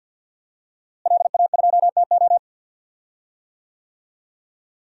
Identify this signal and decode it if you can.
Morse code, heard as a keyed tone, about 51 wpm, 700 Hz. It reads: LA2TO
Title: DXCC-100 Call Signs - 5 Characters - Repeated 2x - 50wpm